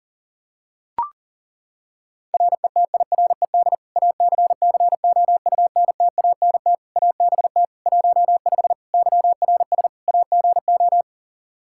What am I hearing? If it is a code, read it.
Morse code, heard as a keyed tone, about 40 wpm, 700 Hz. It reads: RETIRED ACCOUNTANT ABT 15 YRS AGO
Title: Practice QSO - Exchange 3 - More Info - 40wpm